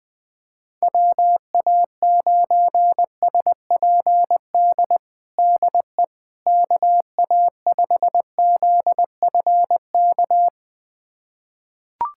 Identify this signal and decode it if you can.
Morse code, heard as a keyed tone, about 21 wpm, 700 Hz. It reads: WA9SPD DE KA5ZFK
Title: USA Callsign DE Callsign - 6 Characters - 20wpm